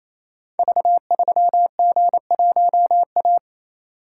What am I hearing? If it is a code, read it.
Morse code, heard as a keyed tone, about 29 wpm, 700 Hz. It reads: V3G1A
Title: Code Groups - 5 Letter:Numbers - 28wpm